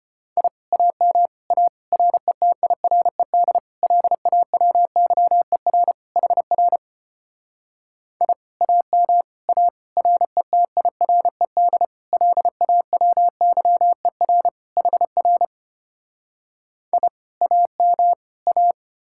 Morse code, heard as a keyed tone, about 33 wpm, 700 Hz. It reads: I AM A RETIRED LAWYER HR I AM A RETIRED LAWYER HR I AM A